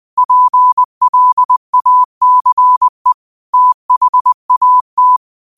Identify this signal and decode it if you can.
Morse code, heard as a keyed tone, about 20 wpm, 1000 Hz. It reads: PLACE THAT
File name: qotdfiles / 20wpm